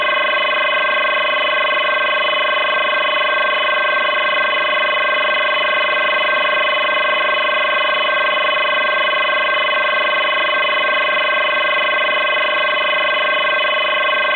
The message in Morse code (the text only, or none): none